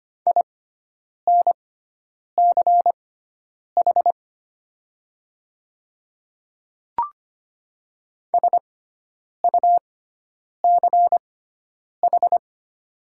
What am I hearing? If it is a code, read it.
Morse code, heard as a keyed tone, about 25 wpm, spaced as for 9 wpm, 700 Hz. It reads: INCH SUCH
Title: 6x Character Spacing - Top 500 Words - 25wpm